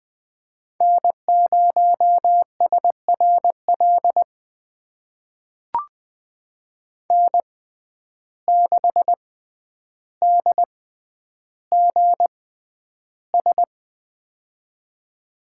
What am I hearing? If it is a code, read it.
Morse code, heard as a keyed tone, about 21 wpm, spaced as for 10 wpm, 700 Hz. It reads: N0SRL N6DGS
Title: USA Call Signs - 5 Characters - 6x Character Spacing - 20wpm